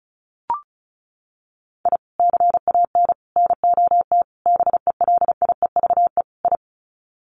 Morse code, heard as a keyed tone, about 34 wpm, 700 Hz. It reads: I CAN NOT BELIEVE I
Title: Sets of 5 Words Encore2 35wpm